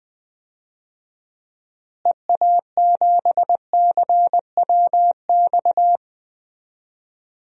Morse code, heard as a keyed tone, about 20 wpm, 700 Hz. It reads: EA7CWX
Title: DXCC-100 Call Signs - 6 Characters - Repeated 2x - 20wpm